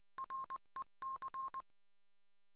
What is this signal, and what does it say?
Morse code, heard as a keyed tone, about 20 wpm, 1060 Hz. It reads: REC